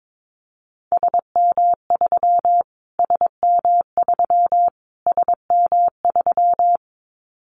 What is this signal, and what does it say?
Morse code, heard as a keyed tone, about 22 wpm, 700 Hz. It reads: SM3 SM3 SM3
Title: Callsign Prefixes for Top 25 DXCC - Repeated 3x - 22wpm